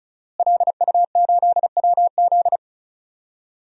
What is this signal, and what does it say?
Morse code, heard as a keyed tone, about 36 wpm, 700 Hz. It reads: LU8WZ
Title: DXCC-100 Call Signs - 5 Characters - Repeated 3x - 35wpm